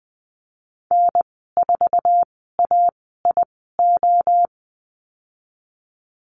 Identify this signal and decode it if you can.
Morse code, heard as a keyed tone, about 20 wpm, spaced as for 14 wpm, 700 Hz. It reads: N4AIO
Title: 076 Lesson 13g - Letter L - Call Signs 20wpm